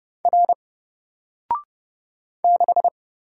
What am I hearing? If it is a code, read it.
Morse code, heard as a keyed tone, about 30 wpm, 700 Hz. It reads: R 6
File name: Single-Letter-Number-30wpm